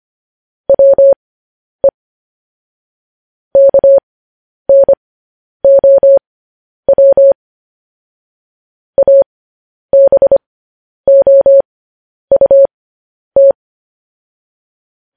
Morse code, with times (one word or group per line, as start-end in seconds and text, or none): 0.7-1.9 WE
3.6-7.3 KNOW
9.0-13.5 ABOUT